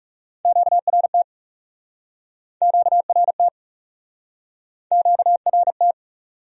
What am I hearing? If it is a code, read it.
Morse code, heard as a keyed tone, about 41 wpm, 700 Hz. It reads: QRT QRT QRT